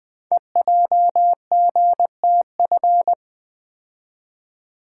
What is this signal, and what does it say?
Morse code, heard as a keyed tone, about 20 wpm, 700 Hz. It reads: EJGTF
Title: Code Groups - 5 Letters - 20wpm